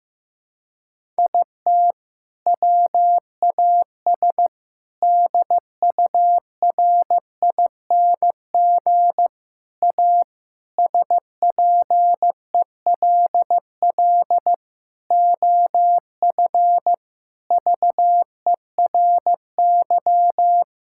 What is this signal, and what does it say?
Morse code, heard as a keyed tone, about 15 wpm, 700 Hz. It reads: IT WAS DURING A SPELL OF VERY
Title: Sentences from Top 1000 Words - Review Part 5 - 15wpm